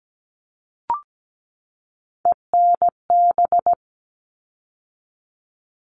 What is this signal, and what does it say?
Morse code, heard as a keyed tone, about 17 wpm, 700 Hz. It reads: ENB
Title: All Cut Numbers - 3-Digit 17wpm